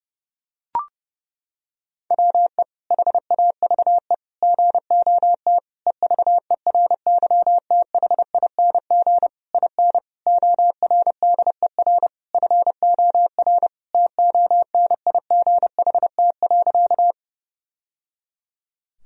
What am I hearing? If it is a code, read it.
Morse code, heard as a keyed tone, about 32 wpm, 700 Hz. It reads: WE HAVE GOT EVERYTHING IN ORDER FOR TONIGHT.
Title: Sentences from Top 1000 Words - Review Part 3 - 30wpm